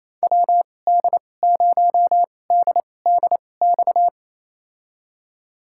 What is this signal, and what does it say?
Morse code, heard as a keyed tone, about 28 wpm, spaced as for 20 wpm, 700 Hz. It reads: WD0DDX
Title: USA Call Signs - 6 Characters - 2x Character Spacing - 28wpm